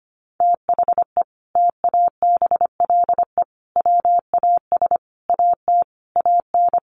Morse code, heard as a keyed tone, about 25 wpm, 700 Hz. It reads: THE TABLE WAS AT AN
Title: Sentences from Top 1500 Words 25wpm